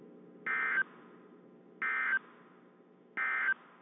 Emergency Alert System (SAME) tones, heard as harsh buzzing bursts. It is heard from 0.4 to 3.5 s.
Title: WZ2523 - NOAA Weather Radio